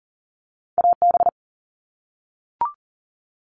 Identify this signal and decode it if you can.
Morse code, heard as a keyed tone, about 40 wpm, 700 Hz. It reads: AB
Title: Canadian Provinces and Territories - Repeated 3x - 40wpm